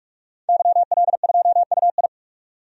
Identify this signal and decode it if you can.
Morse code, heard as a keyed tone, about 46 wpm, 700 Hz. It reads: YL2UI
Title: DXCC-100 Call Signs - 5 Characters - 45wpm